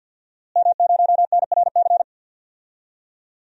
Morse code, heard as a keyed tone, about 53 wpm, 700 Hz. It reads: M0NRC
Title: DXCC-100 Call Signs - 5 Characters - Repeated 3x - 50wpm